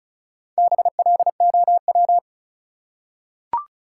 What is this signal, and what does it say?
Morse code, heard as a keyed tone, about 36 wpm, 700 Hz. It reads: BLOW
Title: Four Letter Words - 35wpm